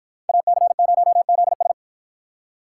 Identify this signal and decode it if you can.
Morse code, heard as a keyed tone, about 50 wpm, 700 Hz. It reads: AC0ZS